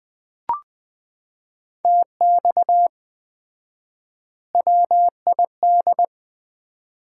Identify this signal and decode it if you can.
Morse code, heard as a keyed tone, about 19 wpm, 700 Hz. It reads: TX WID
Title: Common QSO Phrases - 4x Word-Spacing - 20wpm